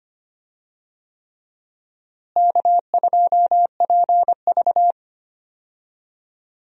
Morse code, heard as a keyed tone, about 25 wpm, 700 Hz. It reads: K2PV
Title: USA Call Signs - 4 Characters - Repeated 3x - 25wpm